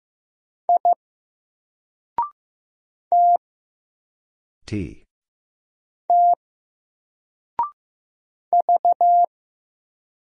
Morse code, heard as a keyed tone, about 15 wpm, 700 Hz. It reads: I T T V
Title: Single Character (Letter:Number:Punctuation:Prosign) 15wpm